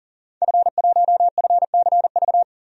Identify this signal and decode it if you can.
Morse code, heard as a keyed tone, about 40 wpm, 700 Hz. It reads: F1FCV
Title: DXCC-100 Call Signs - 5 Characters - 40wpm